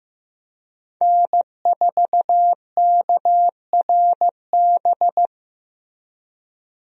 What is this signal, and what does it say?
Morse code, heard as a keyed tone, about 16 wpm, 700 Hz. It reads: N4KRB
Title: USA Call Signs - 5 Characters - Repeated 2x - 15wpm